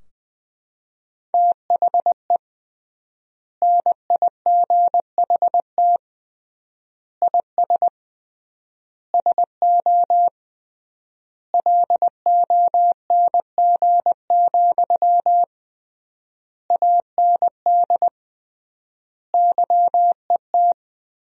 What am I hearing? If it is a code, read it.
Morse code, heard as a keyed tone, about 21 wpm, 700 Hz. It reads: THE NIGHT IS SO LONG, AND YET
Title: 3x Word-Spacing - Sentences from Top 500 Words - 20wpm